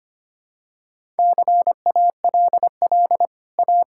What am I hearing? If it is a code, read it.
Morse code, heard as a keyed tone, about 25 wpm, 700 Hz. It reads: CALL A
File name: Easy Sentences with Hints - 25wpm